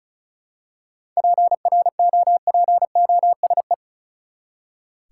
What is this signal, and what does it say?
Morse code, heard as a keyed tone, about 34 wpm, 700 Hz. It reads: PROPOSE
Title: Seven Letter Words - 35wpm